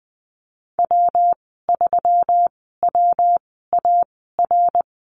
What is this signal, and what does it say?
Morse code, heard as a keyed tone, about 20 wpm, spaced as for 15 wpm, 700 Hz. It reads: W3WAR
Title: 173 Lesson 28g - Number 9 - Call Signs 20wpm